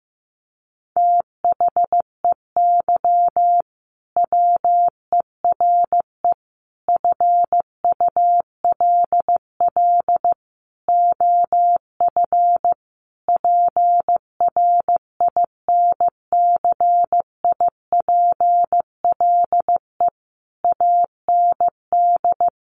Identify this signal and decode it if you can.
Morse code, heard as a keyed tone, about 15 wpm, 700 Hz. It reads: THEY WERE FULL OF PRINCIPLE AND
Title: Sentences from Top 1400 Words 15wpm